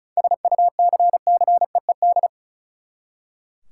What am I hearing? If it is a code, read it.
Morse code, heard as a keyed tone, about 36 wpm, 700 Hz. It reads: SUCCEED
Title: Seven Letter Words - 35wpm